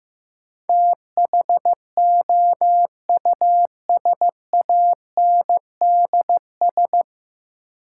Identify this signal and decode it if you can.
Morse code, heard as a keyed tone, about 16 wpm, 700 Hz. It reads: THOUSANDS
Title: Sets of 4 Words - New Word Drill - 15wpm